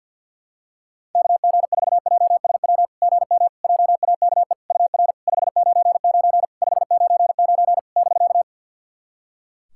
Morse code, heard as a keyed tone, about 50 wpm, 700 Hz. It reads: KG4JSW GM JAKE UR 599 599 <BK>
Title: SOTA QSO Practice 50wpm